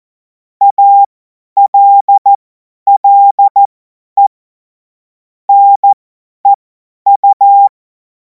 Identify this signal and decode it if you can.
Morse code, heard as a keyed tone, about 14 wpm, spaced as for 10 wpm, 800 Hz. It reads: ALLE NEU
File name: Morsekurs / Witze / Witz 24. Februar 2026 50 F70